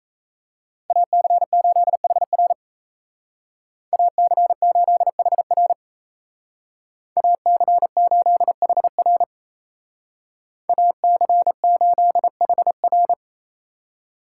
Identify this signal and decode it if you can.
Morse code, heard as a keyed tone, about 33 wpm, 700 Hz. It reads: AC8HR AC8HR AC8HR AC8HR
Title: Speed Racing - USA 2x2 Call Signs 28wpm